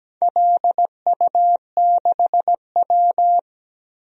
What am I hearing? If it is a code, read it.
Morse code, heard as a keyed tone, about 17 wpm, 700 Hz. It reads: LU6W
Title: DXCC-100 Call Signs - 4 Characters - Repeated 2x - 17wpm